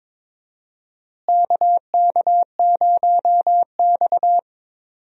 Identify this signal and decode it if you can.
Morse code, heard as a keyed tone, about 22 wpm, 700 Hz. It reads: KK0X
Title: USA Call Signs - 4 Characters - Repeated 3x - 22wpm